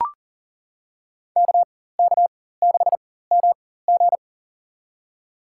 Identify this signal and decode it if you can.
Morse code, heard as a keyed tone, about 40 wpm, spaced as for 19 wpm, 700 Hz. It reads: KK6MG
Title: USA Call Signs - 5 Characters - 4x Character Spacing - 40wpm